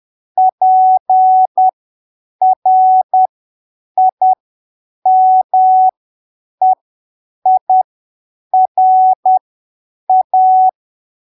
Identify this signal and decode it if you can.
Morse code, heard as a keyed tone, about 10 wpm, spaced as for 7 wpm, 750 Hz. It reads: PRIMEIRA
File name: arquivos / texto / texto 29 10ppm